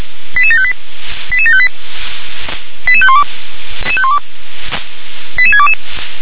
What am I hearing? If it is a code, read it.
Five-tone selective call (ZVEI), heard as quick runs of five tones, tones of about 70 ms. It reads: 88758 88758 88512 88622
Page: /357/SDR/ZVEI/